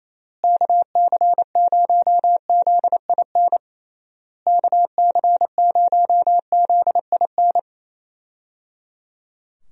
Morse code, heard as a keyed tone, about 27 wpm, 700 Hz. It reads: KC0ZIN KC0ZIN
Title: USA Call Signs - 6 Characters - Repeated 3x - 28wpm